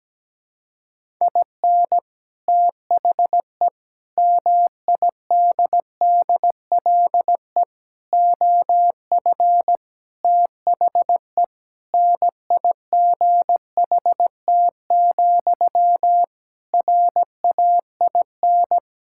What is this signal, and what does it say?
Morse code, heard as a keyed tone, about 17 wpm, 700 Hz. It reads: IN THE MIDDLE OF THE NIGHT, RAIN
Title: Sentences from Top 1000 Words - Review Part 6 - 17wpm